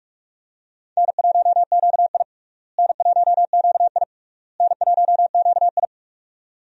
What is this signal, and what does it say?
Morse code, heard as a keyed tone, about 46 wpm, 700 Hz. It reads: N1QI N1QI N1QI
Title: USA Call Signs - 4 Characters - Repeated 3x - 45wpm